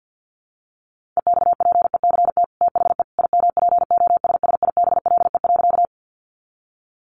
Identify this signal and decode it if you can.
Morse code, heard as a keyed tone, about 50 wpm, 700 Hz. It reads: EXPECT THE IMPOSSIBLE.
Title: Sentences from Top 1100 Words 50wpm